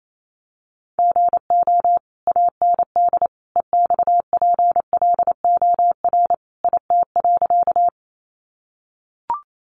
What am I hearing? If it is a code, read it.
Morse code, heard as a keyed tone, about 28 wpm, 700 Hz. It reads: GO AND EXPLOR IT.
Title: Sentences from Top 1100 Words 28wpm